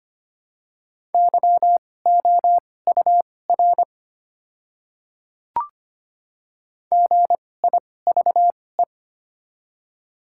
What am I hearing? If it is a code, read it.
Morse code, heard as a keyed tone, about 25 wpm, spaced as for 18 wpm, 690 Hz. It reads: YOUR GIVE
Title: 2x Character Spacing - Top 500 Words - 25wpm